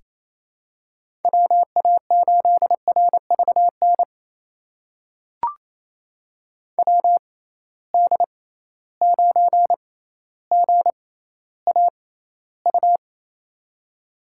Morse code, heard as a keyed tone, about 28 wpm, spaced as for 14 wpm, 700 Hz. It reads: WA8RVN WD9GAU
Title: USA Call Signs - 6x Character Spacing - 28wpm